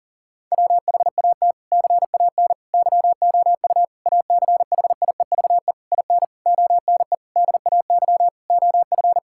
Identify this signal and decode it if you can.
Morse code, heard as a keyed tone, about 37 wpm, 700 Hz. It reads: WHAT CAN YOU ACHIEVE IN ONE DAY OF
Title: Sentences from Top 1000 Words - Review Part 6 - 40wpm